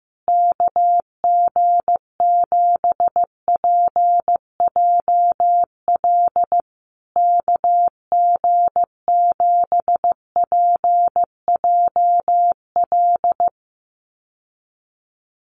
Morse code, heard as a keyed tone, about 15 wpm, 700 Hz. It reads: KG7PJL KG7PJL
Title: USA Call Signs Repeated 3x 15wpm